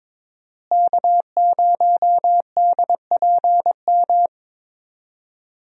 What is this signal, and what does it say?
Morse code, heard as a keyed tone, about 22 wpm, 700 Hz. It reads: K0DPM
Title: USA Call Signs - 5 Characters - Repeated 2x - 22wpm